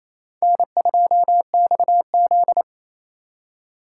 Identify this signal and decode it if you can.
Morse code, heard as a keyed tone, about 34 wpm, 700 Hz. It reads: N2XZ
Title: Speed Racing - USA 1x2 Call Signs 28wpm